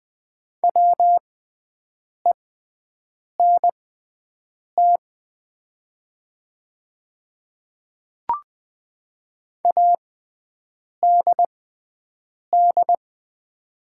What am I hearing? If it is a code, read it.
Morse code, heard as a keyed tone, about 20 wpm, spaced as for 7 wpm, 700 Hz. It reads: WENT ADD
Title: 6x Character Spacing - Top 200 Words - 20wpm